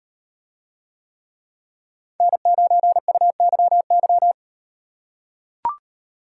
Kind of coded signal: Morse code

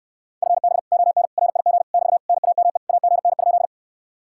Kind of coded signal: Morse code

